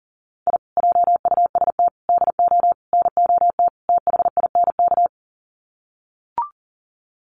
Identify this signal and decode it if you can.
Morse code, heard as a keyed tone, about 40 wpm, 700 Hz. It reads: I JUST DO NOT THINK
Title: Sets of 5 Words Encore 40wpm